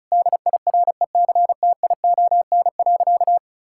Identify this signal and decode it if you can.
Morse code, heard as a keyed tone, about 36 wpm, 700 Hz. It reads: DIRECTION.
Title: Sentences from Top 1000 Words - Review Part 5 - 35wpm